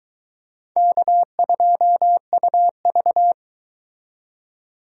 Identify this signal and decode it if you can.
Morse code, heard as a keyed tone, about 23 wpm, 700 Hz. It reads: K2UV